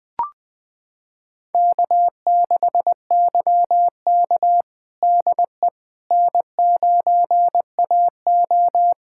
Morse code, heard as a keyed tone, about 20 wpm, 700 Hz. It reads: K6YK DE N9AO
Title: USA Callsign DE Callsign - 4 Characters - 20wpm